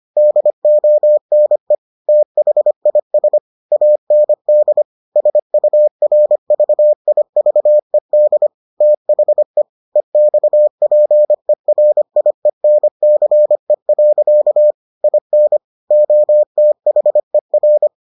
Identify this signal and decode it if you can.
Morse code, heard as a keyed tone, about 25 wpm, 600 Hz. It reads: DONE THIS AND SURVIVED THE EXPERIENCE. IN OTHER